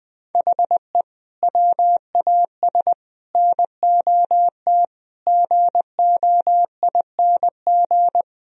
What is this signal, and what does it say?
Morse code, heard as a keyed tone, about 21 wpm, 700 Hz. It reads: HE WAS NOT GOING
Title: Speed-Racing - Sets of 5 Words 20wpm